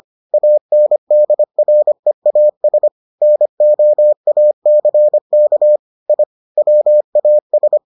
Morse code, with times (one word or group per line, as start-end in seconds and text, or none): 0.3-2.9 ANDREAS
3.2-5.8 NOACK
6.1-6.2 I
6.6-7.8 WAS